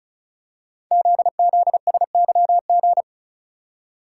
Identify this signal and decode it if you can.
Morse code, heard as a keyed tone, about 35 wpm, 700 Hz. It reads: ZZSYG
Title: Code Groups - 5 Letters - 35wpm